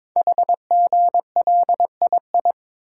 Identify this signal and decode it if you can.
Morse code, heard as a keyed tone, about 22 wpm, 700 Hz. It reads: HGLII
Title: Code Groups - 5 Letters - 22wpm